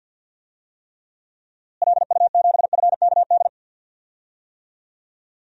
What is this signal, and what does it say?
Morse code, heard as a keyed tone, about 47 wpm, 700 Hz. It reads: LU7FKD
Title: DXCC-100 Call Signs - 6 Characters - Repeated 2x - 50wpm